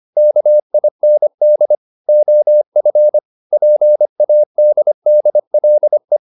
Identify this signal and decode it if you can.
Morse code, heard as a keyed tone, about 25 wpm, 600 Hz. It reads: KIND OF PADDLE